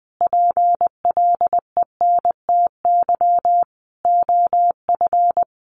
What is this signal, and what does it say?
Morse code, heard as a keyed tone, about 19 wpm, 700 Hz. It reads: PLENTY OF
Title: Sentences from Top 1300 Words 20wpm